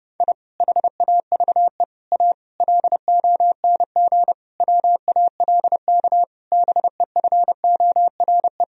Morse code, heard as a keyed tone, about 30 wpm, 700 Hz. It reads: I HAVE A LONG WALK BEFORE
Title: Sentences from Top 1000 Words - Review Part 1 - 30wpm